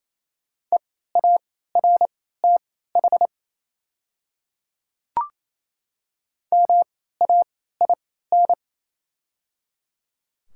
Morse code, heard as a keyed tone, about 28 wpm, spaced as for 16 wpm, 700 Hz. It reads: EARTH MAIN
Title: 3x Character Spacing - Top 300 Words - 28wpm